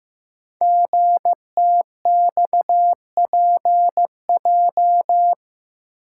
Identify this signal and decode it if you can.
Morse code, heard as a keyed tone, about 15 wpm, 700 Hz. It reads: GTXPJ
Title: Code Groups - 5 Letters - 15wpm